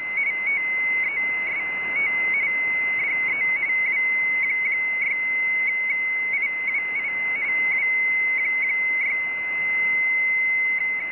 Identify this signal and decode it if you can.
RTTY, about 45.45 baud. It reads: TEST DE W6IWI W6
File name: rtty / tu170 / websdr recording 7113.0kHz